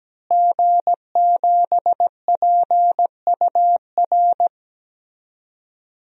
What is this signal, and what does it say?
Morse code, heard as a keyed tone, about 17 wpm, 700 Hz. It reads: G7PUR
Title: DXCC-100 Call Signs - 5 Characters - Repeated 3x - 17wpm